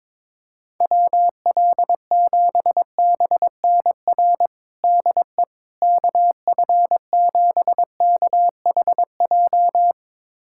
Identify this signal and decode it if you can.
Morse code, heard as a keyed tone, about 22 wpm, 700 Hz. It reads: WL7BNR DE KF7KHJ
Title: USA Callsign DE Callsign - 6 Characters - 22wpm